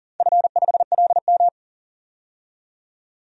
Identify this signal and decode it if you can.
Morse code, heard as a keyed tone, about 38 wpm, 700 Hz. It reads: F5LM
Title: DXCC-100 Call Signs - 4 Characters - 40wpm